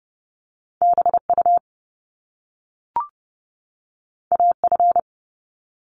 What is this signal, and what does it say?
Morse code, heard as a keyed tone, about 30 wpm, 700 Hz. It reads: BU AF
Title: Top Two Letter Combinations in Words 30wpm